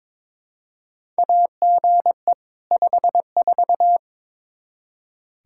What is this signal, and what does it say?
Morse code, heard as a keyed tone, about 22 wpm, 700 Hz. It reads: AGE 54
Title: Practice QSO - Exchange 3 - More Info - 22wpm